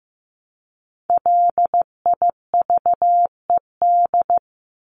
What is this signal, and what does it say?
Morse code, heard as a keyed tone, about 15 wpm, 700 Hz. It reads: LIVED
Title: ED Suffix - Top 100 Words with Base Words 15wpm